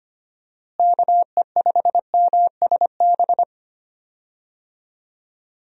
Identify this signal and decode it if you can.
Morse code, heard as a keyed tone, about 25 wpm, 700 Hz. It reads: KE5MSB
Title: USA Call Signs - 6 Characters - Repeated 3x - 25wpm